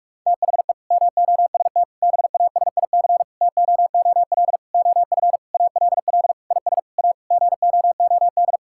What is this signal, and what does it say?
Morse code, heard as a keyed tone, about 46 wpm, 700 Hz. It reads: THE MOST BASIC TOOL OF ALL IS A GOOD
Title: Sentences from Top 1000 Words - Review Part 6 - 45wpm